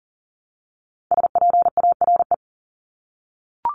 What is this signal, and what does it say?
Morse code, heard as a keyed tone, about 40 wpm, 700 Hz. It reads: SPARE
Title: Top 1751-2000 Words 40wpm